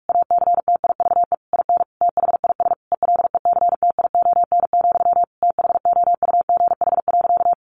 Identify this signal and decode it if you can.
Morse code, heard as a keyed tone, about 44 wpm, 700 Hz. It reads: ACTIVE IN THIS ELECTION, THOUGH.
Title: Sentences from Top 1500 Words 45wpm